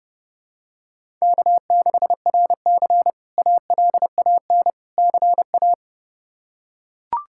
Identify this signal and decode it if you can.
Morse code, heard as a keyed tone, about 30 wpm, 700 Hz. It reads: K6RC ALAN CA
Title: K1USN SST QSO Practice - 30wpm